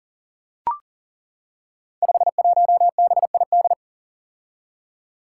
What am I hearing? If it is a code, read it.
Morse code, heard as a keyed tone, about 38 wpm, 700 Hz. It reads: 51BID
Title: Five Challenging Letter-Number Combinations 40wpm